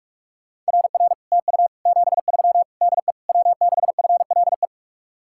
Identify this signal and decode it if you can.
Morse code, heard as a keyed tone, about 59 wpm, 700 Hz. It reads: RR TU 73 DE W6FLE